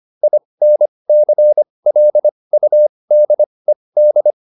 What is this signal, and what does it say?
Morse code, heard as a keyed tone, about 25 wpm, spaced as for 20 wpm, 600 Hz. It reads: INCLUDED
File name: IN8 20